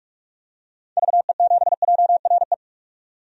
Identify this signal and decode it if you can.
Morse code, heard as a keyed tone, about 46 wpm, 700 Hz. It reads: VE7JRE
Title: DXCC-100 Call Signs - 6 Characters - 45wpm